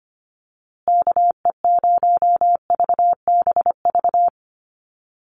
Morse code, heard as a keyed tone, about 25 wpm, 700 Hz. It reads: KE0VBV
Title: USA Call Signs Repeated 3x 25wpm